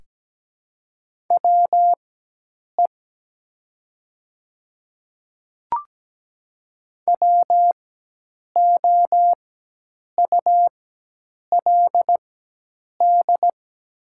Morse code, heard as a keyed tone, about 17 wpm, spaced as for 8 wpm, 690 Hz. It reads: WE WOULD
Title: 4x Character Spacing - Top 100 Words - 17wpm